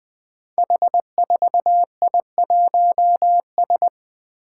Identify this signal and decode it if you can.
Morse code, heard as a keyed tone, about 20 wpm, 700 Hz. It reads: H4I1S
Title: Five Challenging Letter-Number Combinations 20wpm